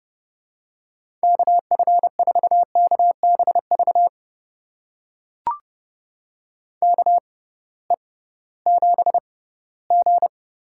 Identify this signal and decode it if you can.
Morse code, heard as a keyed tone, about 32 wpm, spaced as for 18 wpm, 700 Hz. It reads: KF4KBV KE7G
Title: USA Call Signs - 6x Character Spacing - 30wpm